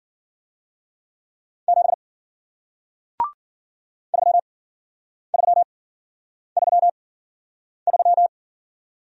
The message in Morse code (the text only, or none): 7 3333